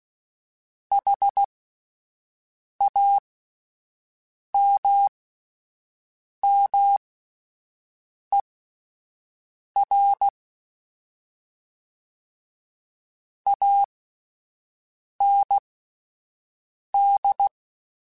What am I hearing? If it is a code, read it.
Morse code, heard as a keyed tone, about 16 wpm, spaced as for 5 wpm, 800 Hz. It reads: HAMMER AND
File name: Misc / Misc-mp3s / WordPairs sp @10